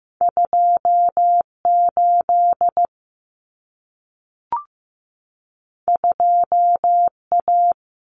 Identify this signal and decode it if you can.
Morse code, heard as a keyed tone, about 15 wpm, 700 Hz. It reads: 28 2A
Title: Common Cut Numbers - 2-Digit 15wpm